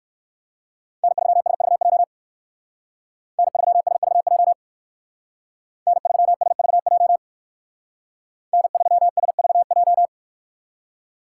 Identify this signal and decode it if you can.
Morse code, heard as a keyed tone, about 56 wpm, 700 Hz. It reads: N3SVJ N3SVJ N3SVJ N3SVJ
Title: Speed Racing - USA 1x3 Call Signs 45wpm